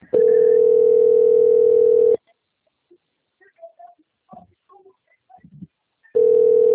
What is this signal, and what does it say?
Ringback tone (North America), 440 + 480 Hz.